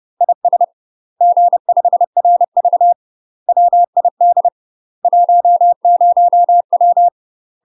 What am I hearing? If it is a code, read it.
Morse code, heard as a keyed tone, about 30 wpm, 700 Hz. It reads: IS G5RV WID 10W
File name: UK MASTER QSOs TRAINING / 30WPM / UK MASTER QSOs 0007